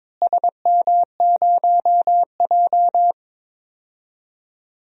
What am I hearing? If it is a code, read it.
Morse code, heard as a keyed tone, about 22 wpm, 700 Hz. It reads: SM0J
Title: DXCC-100 Call Signs - 4 Characters - Repeated 3x - 22wpm